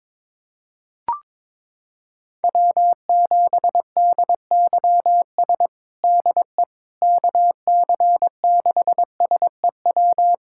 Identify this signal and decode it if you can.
Morse code, heard as a keyed tone, about 22 wpm, 700 Hz. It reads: W7DYS DE KC6SEW
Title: USA Callsign DE Callsign 22wpm